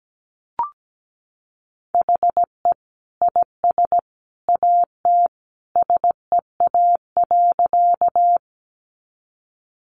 Morse code, heard as a keyed tone, about 17 wpm, 700 Hz. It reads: HE IS AT SEA.
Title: Sentences from Top 300 Words 17wpm